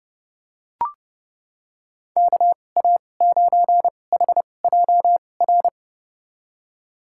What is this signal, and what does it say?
Morse code, heard as a keyed tone, about 30 wpm, spaced as for 22 wpm, 700 Hz. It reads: KA9HJR
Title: USA Call Signs - 6 Characters - 2x Character Spacing - 30wpm